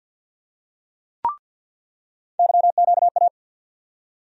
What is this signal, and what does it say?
Morse code, heard as a keyed tone, about 47 wpm, 700 Hz. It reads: YQA
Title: USA Call Sign Postfixes - 3 Letters 50wpm